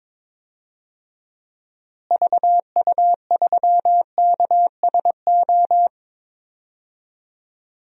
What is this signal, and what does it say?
Morse code, heard as a keyed tone, about 22 wpm, 700 Hz. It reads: VU3KSO
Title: DXCC-100 Call Signs - 6 Characters - Repeated 2x - 22wpm